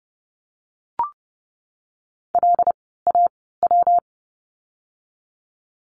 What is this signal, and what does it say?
Morse code, heard as a keyed tone, about 30 wpm, 700 Hz. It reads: L A W
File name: course / 30:10wpm / 115 Lesson 19f - Letter w - Words 30wpm